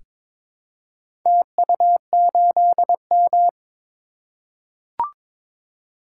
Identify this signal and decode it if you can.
Morse code, heard as a keyed tone, about 22 wpm, 700 Hz. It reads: TU8M
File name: practice / 22wpm / Four Letter-Number Combinations - 22wpm